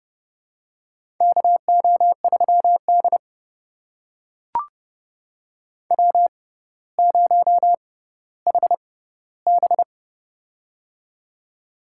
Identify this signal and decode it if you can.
Morse code, heard as a keyed tone, about 30 wpm, spaced as for 15 wpm, 700 Hz. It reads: KO3D W0HB
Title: USA Call Signs - 4 Characters - 6x Character Spacing - 30wpm